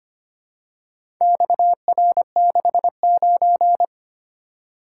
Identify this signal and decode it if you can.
Morse code, heard as a keyed tone, about 25 wpm, 700 Hz. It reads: XR69
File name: practice / 25wpm / Four Letter-Number Combinations - 25wpm